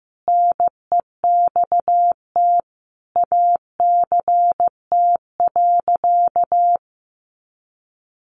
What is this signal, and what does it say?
Morse code, heard as a keyed tone, about 14 wpm, 700 Hz. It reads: NEXT ACT.
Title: Sentences from Top 900 Words 15wpm